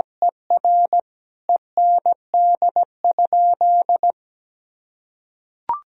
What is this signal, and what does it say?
Morse code, heard as a keyed tone, about 17 wpm, 700 Hz. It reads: ER END?
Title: Sentences from Top 1000 Words - Review Part 4 - 17wpm